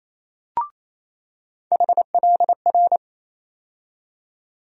Morse code, heard as a keyed tone, about 28 wpm, 700 Hz. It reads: HLR
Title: USA Call Sign Postfixes - 3 Letters 28wpm